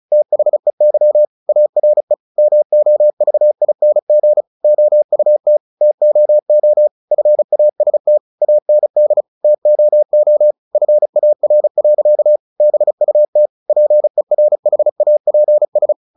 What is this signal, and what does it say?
Morse code, heard as a keyed tone, about 34 wpm, 600 Hz. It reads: THEY ARE MOVING OUT TOO FAST AND TOO FAR. BUT PERHAPS
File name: the old man and the sea / the old man and the sea 1 35wpm